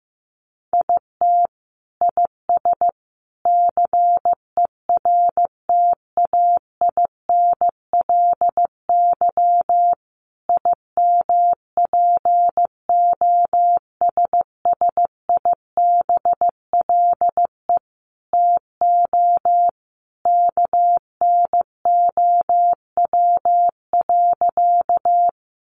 Morse code, heard as a keyed tone, about 15 wpm, 700 Hz. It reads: IT IS CERTAINLY IMPOSSIBLE TO KNOW.
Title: Sentences from Top 1100 Words 15wpm